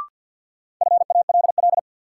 Morse code, heard as a keyed tone, about 52 wpm, 700 Hz. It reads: FALL
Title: Sentences from Top 500 Words - New Word Drill - 50wpm